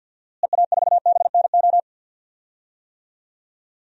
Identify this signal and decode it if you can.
Morse code, heard as a keyed tone, about 47 wpm, 700 Hz. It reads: EA4BNO
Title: DXCC-100 Call Signs - 6 Characters - Repeated 2x - 50wpm